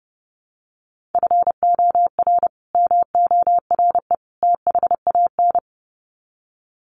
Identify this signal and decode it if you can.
Morse code, heard as a keyed tone, about 30 wpm, 700 Hz. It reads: FOR MORE THAN
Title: Sets of 3 Words 30wpm